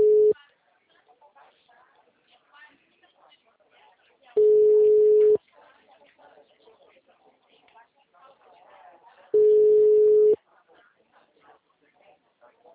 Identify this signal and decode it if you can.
Ringback tone (Europe, 425 Hz), with speech.